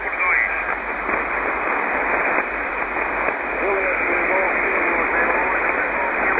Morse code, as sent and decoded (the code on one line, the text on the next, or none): none